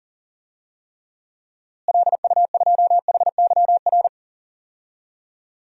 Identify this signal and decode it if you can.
Morse code, heard as a keyed tone, about 40 wpm, 700 Hz. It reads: LU2HYR
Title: DXCC-100 Call Signs - 6 Characters - Repeated 3x - 40wpm